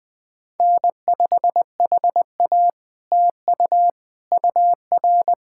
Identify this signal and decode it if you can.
Morse code, heard as a keyed tone, about 19 wpm, 700 Hz. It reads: N5HA TU UR
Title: POTA QSO Practice - 20wpm